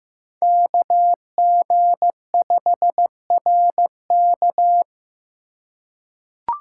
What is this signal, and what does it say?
Morse code, heard as a keyed tone, about 15 wpm, 700 Hz. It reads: KG5RK
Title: USA Call Signs - 2x Character Spacing - 15wpm